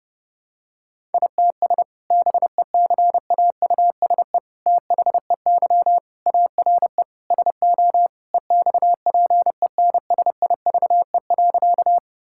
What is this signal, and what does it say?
Morse code, heard as a keyed tone, about 32 wpm, 700 Hz. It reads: ITS BECAUSE THEY ARE SO EXPENSIVE.
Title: Sentences from Top 1000 Words - Review Part 6 - 30wpm